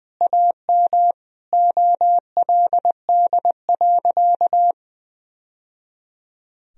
Morse code, heard as a keyed tone, about 20 wpm, 700 Hz. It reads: AM OLD.
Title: Sentences from Top 1000 Words - Review Part 3 - 20wpm